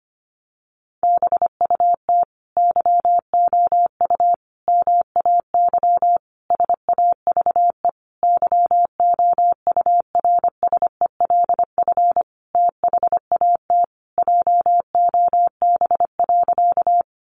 Morse code, heard as a keyed tone, about 25 wpm, 700 Hz. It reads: BUT YOU MAY SAVE YOURSELF THAT JOB.
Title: Sentences from Top 1100 Words 25wpm